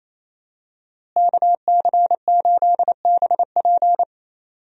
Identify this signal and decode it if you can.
Morse code, heard as a keyed tone, about 28 wpm, 700 Hz. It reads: KC8BP
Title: USA Call Signs - 5 Characters - Repeated 2x - 28wpm